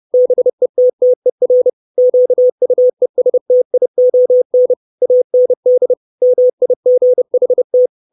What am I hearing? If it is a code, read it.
Morse code, heard as a keyed tone, about 30 wpm, 500 Hz. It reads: BETTER QUESTION AND MIGHT
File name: 632 Words 30WPM